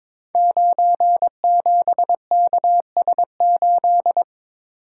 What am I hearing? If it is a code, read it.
Morse code, heard as a keyed tone, about 22 wpm, 700 Hz. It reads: 97KS8
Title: Code Groups - 5 Letter:Numbers - 22wpm